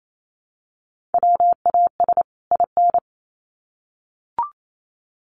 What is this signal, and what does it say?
Morse code, heard as a keyed tone, about 28 wpm, 700 Hz. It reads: WAS IN
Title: Sets of 2 Words 28wpm